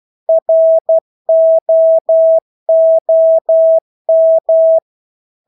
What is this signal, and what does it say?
Morse code, heard as a keyed tone, about 12 wpm, 650 Hz. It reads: ROOM
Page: Engelske ord 12 wpm